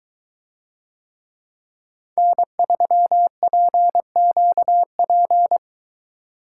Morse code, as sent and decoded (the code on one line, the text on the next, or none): -. ...-- .--. --.- .--.
N3PQP